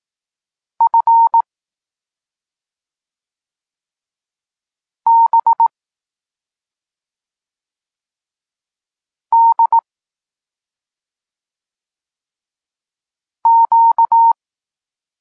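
Morse code, heard as a keyed tone, about 18 wpm, 920 Hz. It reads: F B D Q
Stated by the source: Polish Morse lesson teaching the letter C, from SP9MOA Niepołomicki Klub Krótkofalowców.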